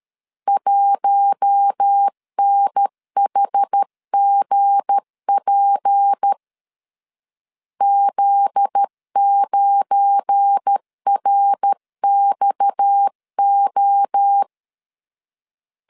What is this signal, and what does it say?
Morse code, heard as a keyed tone, about 13 wpm, 800 Hz. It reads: 1NHGP Z9RXO
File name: LEZIONI / 7 GF9Z / 7 GF9Z 12wpm 2147